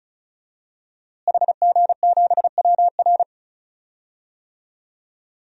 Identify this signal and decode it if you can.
Morse code, heard as a keyed tone, about 36 wpm, 700 Hz. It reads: HG7WR
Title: DXCC-100 Call Signs - 5 Characters - Repeated 2x - 35wpm